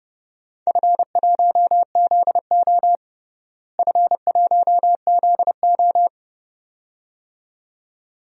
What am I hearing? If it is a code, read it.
Morse code, heard as a keyed tone, about 31 wpm, 700 Hz. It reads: F1ZO F1ZO
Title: DXCC-100 Call Signs - 4 Characters - Repeated 2x - 30wpm